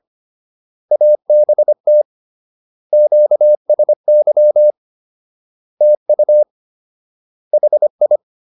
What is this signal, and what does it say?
Morse code, heard as a keyed tone, about 25 wpm, 600 Hz. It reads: ABT QSY TU HI